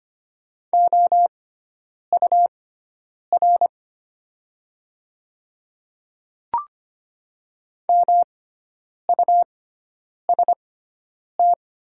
Morse code, heard as a keyed tone, about 25 wpm, spaced as for 9 wpm, 700 Hz. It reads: OUR MUST
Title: 6x Character Spacing - Top 300 Words - 25wpm